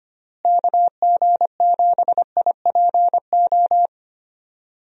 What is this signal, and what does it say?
Morse code, heard as a keyed tone, about 25 wpm, 700 Hz. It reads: KG7IPO
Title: USA Call Signs - 6 Characters - Repeated 2x - 25wpm